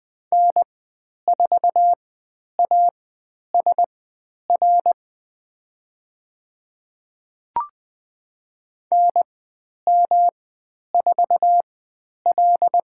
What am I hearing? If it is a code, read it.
Morse code, heard as a keyed tone, about 20 wpm, spaced as for 10 wpm, 700 Hz. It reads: N4ASR NM4L
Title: 109 Lesson 18g - Letter m - Call Signs 20wpm